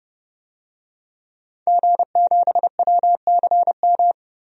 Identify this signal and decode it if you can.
Morse code, heard as a keyed tone, about 30 wpm, 700 Hz. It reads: G7WCM
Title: DXCC-100 Call Signs - 5 Characters - 30wpm